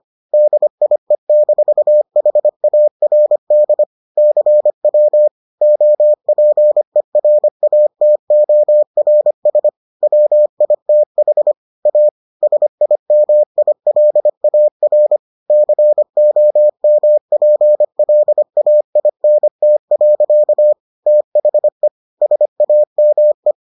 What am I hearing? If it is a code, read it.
Morse code, heard as a keyed tone, about 25 wpm, 600 Hz. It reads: DIE-HARD CW OPERATORS WITH A SIMILAR COMPLAINT. THE SAME